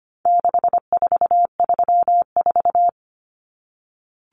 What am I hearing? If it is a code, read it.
Morse code, heard as a keyed tone, about 25 wpm, 700 Hz. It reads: 6434
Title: Numbers - 4-Digit 25wpm